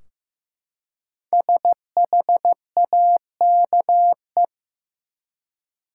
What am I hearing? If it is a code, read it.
Morse code, heard as a keyed tone, about 15 wpm, 700 Hz. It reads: SHAKE
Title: Five Letter Words - 15wpm